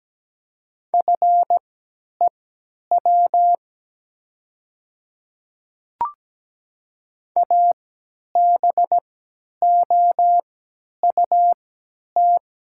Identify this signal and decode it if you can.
Morse code, heard as a keyed tone, about 17 wpm, spaced as for 10 wpm, 690 Hz. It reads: FEW ABOUT
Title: 3x Character Spacing - Top 300 Words - 17wpm